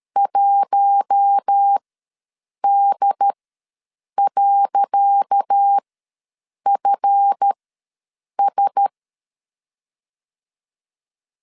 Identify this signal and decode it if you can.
Morse code, heard as a keyed tone, about 13 wpm, spaced as for 7 wpm, 800 Hz. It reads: 1D.FS
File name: LEZIONI / 10 SIGNS / 10C DOT ws 12wpm 3548